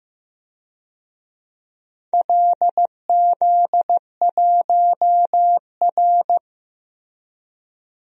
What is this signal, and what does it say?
Morse code, heard as a keyed tone, about 16 wpm, 700 Hz. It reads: LZ1R
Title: DXCC-100 Call Signs - 4 Characters - Repeated 3x - 15wpm